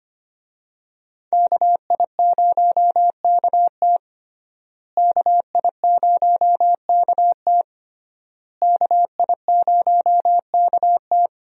Morse code, heard as a keyed tone, about 25 wpm, 700 Hz. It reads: KI0KT KI0KT KI0KT
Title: USA Call Signs - 5 Characters - Repeated 3x - 25wpm